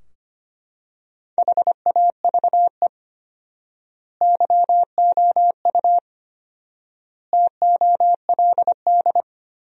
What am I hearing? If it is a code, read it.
Morse code, heard as a keyed tone, about 25 wpm, 700 Hz. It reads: HAVE YOU TOLD
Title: 4x Word-Spacing - Sentences from Top 400 Words - 25wpm